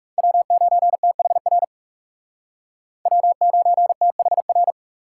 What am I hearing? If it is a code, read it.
Morse code, heard as a keyed tone, about 37 wpm, 700 Hz. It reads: W9THR W9THR
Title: Speed Racing - USA 1x3 Call Signs 30wpm